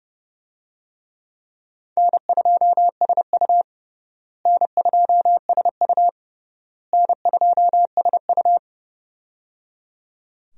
Morse code, heard as a keyed tone, about 30 wpm, 700 Hz. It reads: N2SU N2SU N2SU
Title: USA Call Signs - 4 Characters - Repeated 3x - 30wpm